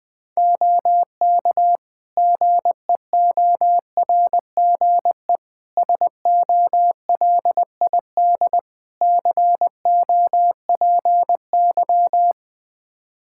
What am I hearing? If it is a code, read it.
Morse code, heard as a keyed tone, about 20 wpm, 700 Hz. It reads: OK GEORGE SOLID COPY
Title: K1USN SST QSO Practice 20wpm